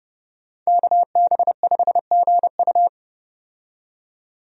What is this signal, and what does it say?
Morse code, heard as a keyed tone, about 30 wpm, 700 Hz. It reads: KB5GU
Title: USA Call Signs - 5 Characters - 30wpm